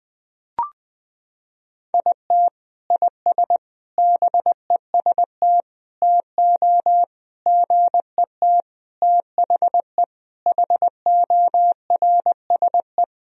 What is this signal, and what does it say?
Morse code, heard as a keyed tone, about 20 wpm, 700 Hz. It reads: IT IS BEST TO GET THE HORSE
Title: Sentences from Top 1000 Words - Review Part 1 - 20wpm